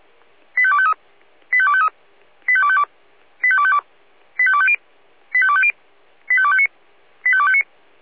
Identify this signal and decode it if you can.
Five-tone selective call (ZVEI), heard as quick runs of five tones, tones of about 70 ms. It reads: 85253 85253 85252 85252 85250 85250 85259 85259